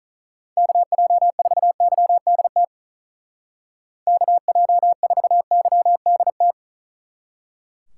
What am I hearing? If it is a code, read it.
Morse code, heard as a keyed tone, about 45 wpm, 700 Hz. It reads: KJ4YDT KJ4YDT